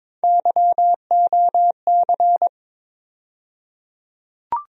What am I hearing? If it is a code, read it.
Morse code, heard as a keyed tone, about 22 wpm, 700 Hz. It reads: YOC
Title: USA Call Sign Postfixes - 3 Letters 22wpm